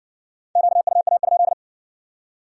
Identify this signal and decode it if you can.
Morse code, heard as a keyed tone, about 55 wpm, 700 Hz. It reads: QUA?